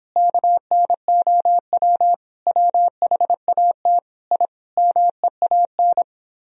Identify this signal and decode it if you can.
Morse code, heard as a keyed tone, about 26 wpm, 700 Hz. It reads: KNOW WHAT I MEAN